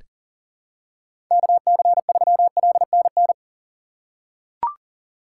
Morse code, heard as a keyed tone, about 40 wpm, 700 Hz. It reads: KC3LNN
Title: USA Call Signs - 6 Characters - 6x Character Spacing - 40wpm